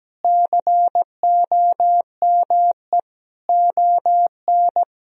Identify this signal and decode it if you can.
Morse code, heard as a keyed tone, about 17 wpm, 700 Hz. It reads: COME ON
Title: Sentences from Top 1000 Words - Review Part 4 - 17wpm